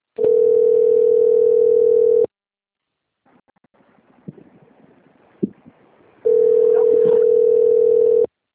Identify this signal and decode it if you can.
Ringback tone (North America), 440 + 480 Hz.
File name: rg-1600-unknown- - - .51340